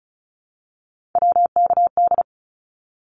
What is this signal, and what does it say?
Morse code, heard as a keyed tone, about 34 wpm, 700 Hz. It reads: WKD
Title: Abbreviations 35wpm